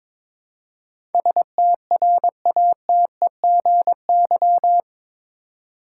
Morse code, heard as a keyed tone, about 22 wpm, 700 Hz. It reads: STRATEGY
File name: practice / 22wpm / Eight Letter Words - 22wpm